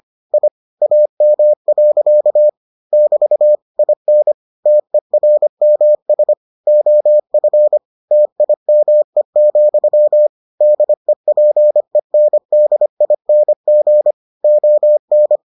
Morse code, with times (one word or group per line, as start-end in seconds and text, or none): 0.3-0.5 I
0.8-2.5 AM.
2.9-3.5 <BT>
3.8-4.3 IN
4.7-6.3 TERMS
6.7-7.8 OF
8.1-10.3 TIME,
10.6-14.1 DEPENDING
14.4-15.4 ON